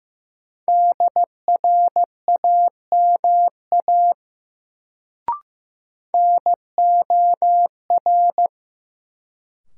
Morse code, heard as a keyed tone, about 15 wpm, 700 Hz. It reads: DRAMA NOR
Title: Sentences from Top 1500 Words - New Word Drill Alt - 15wpm